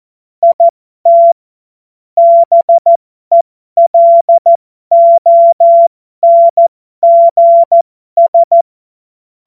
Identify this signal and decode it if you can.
Morse code, heard as a keyed tone, about 14 wpm, spaced as for 12 wpm, 680 Hz. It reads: IT BELONGS
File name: Morsekurs / Witze